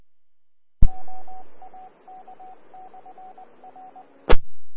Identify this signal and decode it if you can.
Morse code, heard as a keyed tone, about 20 wpm, 750 Hz. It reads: OAK/R E
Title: Austin, TX Amateur Repeaters